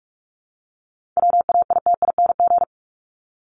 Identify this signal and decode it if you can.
Morse code, heard as a keyed tone, about 44 wpm, 700 Hz. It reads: WAITING
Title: ING Suffix - Top 100 Words with Base Words 45wpm